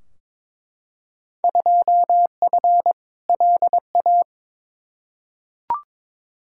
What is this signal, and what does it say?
Morse code, heard as a keyed tone, about 22 wpm, 700 Hz. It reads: 2F LA
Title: Field Day Exchanges - 22wpm